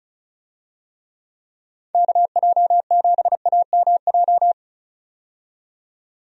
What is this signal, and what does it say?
Morse code, heard as a keyed tone, about 34 wpm, 700 Hz. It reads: KJ7AMJ
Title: USA Call Signs - 6 Characters - 35wpm